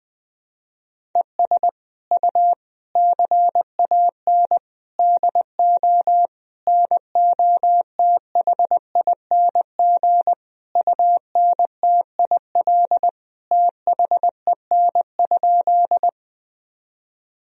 Morse code, heard as a keyed tone, about 21 wpm, 700 Hz. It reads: ES U CAN DO NOTHING UNTIL THEN?
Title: Sentences from Top 500 Words and Abbreviations 20wpm